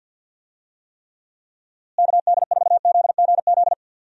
Morse code, heard as a keyed tone, about 52 wpm, 700 Hz. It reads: KD4ZGZ
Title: USA Call Signs - 6 Characters - Repeated 3x - 50wpm